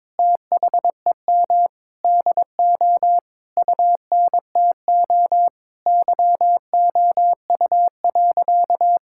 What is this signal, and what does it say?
Morse code, heard as a keyed tone, about 22 wpm, 700 Hz. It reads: THEM DO UNTO YOU.
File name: Sentences with 9-Word Phrases - 22wpm